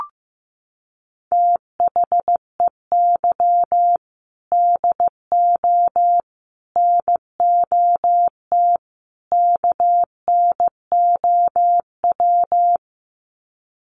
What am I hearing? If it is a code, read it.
Morse code, heard as a keyed tone, about 15 wpm, 700 Hz. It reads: THEY DO NOT KNOW
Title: Sets of 4 Words 15wpm